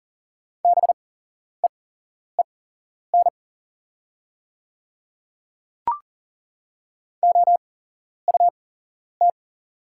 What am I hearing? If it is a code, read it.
Morse code, heard as a keyed tone, about 37 wpm, spaced as for 11 wpm, 690 Hz. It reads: BEEN OUT
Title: 8x Character Spacing - Top 200 Words - 40wpm